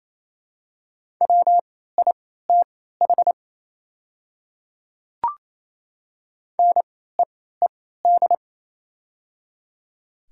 Morse code, heard as a keyed tone, about 27 wpm, spaced as for 16 wpm, 700 Hz. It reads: WITH NEED
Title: 3x Character Spacing - Top 300 Words - 28wpm